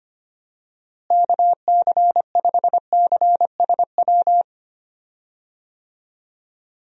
Morse code, heard as a keyed tone, about 25 wpm, 700 Hz. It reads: KC5CSW
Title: USA Call Signs - 6 Characters - Repeated 3x - 25wpm